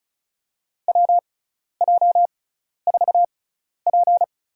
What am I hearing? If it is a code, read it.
Morse code, heard as a keyed tone, about 34 wpm, spaced as for 12 wpm, 700 Hz. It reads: WJ4P
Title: USA Call Signs - 4 Characters - 6x Character Spacing - 35wpm